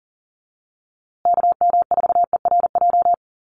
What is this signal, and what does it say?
Morse code, heard as a keyed tone, about 40 wpm, 700 Hz. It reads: KM4ERJ
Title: USA Call Signs Repeated 2x 40wpm